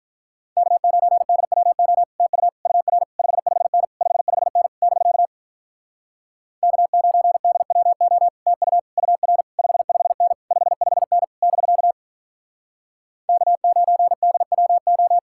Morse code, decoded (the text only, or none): K9DWO TU UR 55N 55N <BK> K9DWO TU UR 55N 55N <BK> K9DWO